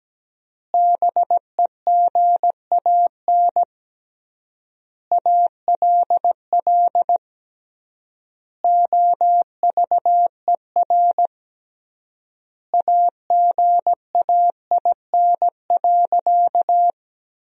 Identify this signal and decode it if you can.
Morse code, heard as a keyed tone, about 17 wpm, 700 Hz. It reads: BEGAN ALL OVER AGAIN.
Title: 3x Word-Spacing - Sentences from Top 500 Words - 17wpm